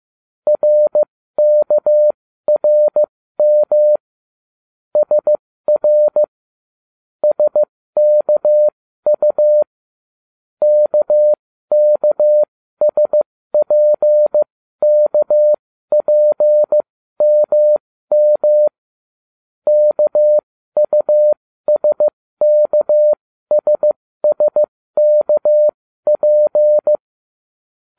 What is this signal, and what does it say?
Morse code, heard as a keyed tone, about 15 wpm, spaced as for 13 wpm, 600 Hz. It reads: RKRM SR SKU KKSPKPMM KUSKSSKP